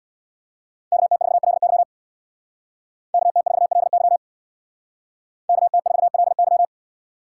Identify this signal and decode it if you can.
Morse code, heard as a keyed tone, about 61 wpm, 700 Hz. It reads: KT4LY KT4LY KT4LY
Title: Speed Racing - USA 2x2 Call Signs 50wpm